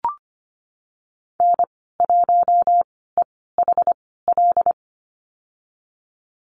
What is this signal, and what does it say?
Morse code, heard as a keyed tone, about 25 wpm, spaced as for 16 wpm, 700 Hz. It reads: N1EHL
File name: Lesson 15g - Number 5 - Call Signs 25wpm